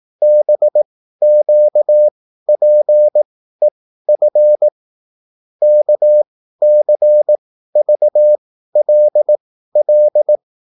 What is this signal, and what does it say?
Morse code, heard as a keyed tone, about 18 wpm, spaced as for 13 wpm, 600 Hz. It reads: BQPEF KCVLL